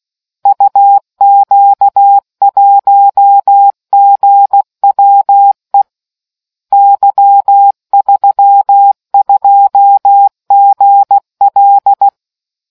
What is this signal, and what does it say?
Morse code, heard as a keyed tone, about 16 wpm, 800 Hz. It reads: UQ1GWE Y32GL